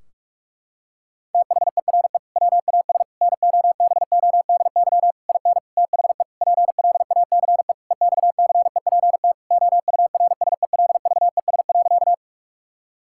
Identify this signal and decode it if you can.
Morse code, heard as a keyed tone, about 42 wpm, 700 Hz. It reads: THERE WAS NOBODY IN THE PLACE EXCEPT OURSELVES.
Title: Sentences from Top 1600 Words 45wpm